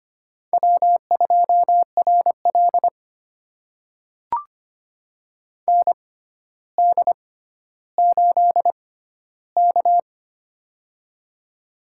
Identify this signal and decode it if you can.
Morse code, heard as a keyed tone, about 25 wpm, spaced as for 13 wpm, 700 Hz. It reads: W2RL ND8K
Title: USA Call Signs - 4 Characters - 6x Character Spacing - 25wpm